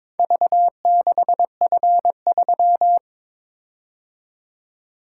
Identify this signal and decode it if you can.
Morse code, heard as a keyed tone, about 21 wpm, 700 Hz. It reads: V6F3
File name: practice / 22wpm / Four Letter-Number Combinations - 22wpm